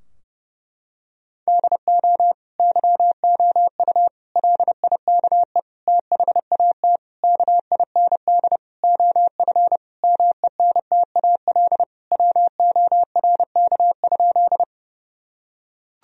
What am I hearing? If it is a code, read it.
Morse code, heard as a keyed tone, about 30 wpm, 700 Hz. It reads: DO YOU LIKE THAT KIND OF MENTAL WORK?
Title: Sentences from Top 1000 Words - Review Part 4 - 30wpm